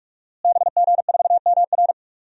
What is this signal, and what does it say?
Morse code, heard as a keyed tone, about 43 wpm, 700 Hz. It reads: BG4MR
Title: DXCC-100 Call Signs - 5 Characters - Repeated 2x - 45wpm